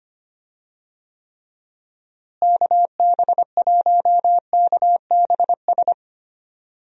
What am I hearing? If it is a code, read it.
Morse code, heard as a keyed tone, about 25 wpm, 700 Hz. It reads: KB1KBS
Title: USA Call Signs - 6 Characters - Repeated 3x - 25wpm